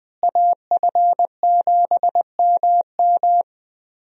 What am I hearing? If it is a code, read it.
Morse code, heard as a keyed tone, about 21 wpm, 700 Hz. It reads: AF7MM
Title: USA Call Signs - 5 Characters - Repeated 2x - 20wpm